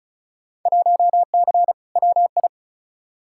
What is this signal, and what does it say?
Morse code, heard as a keyed tone, about 34 wpm, 700 Hz. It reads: 1C WI
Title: Field Day Exchanges - 35wpm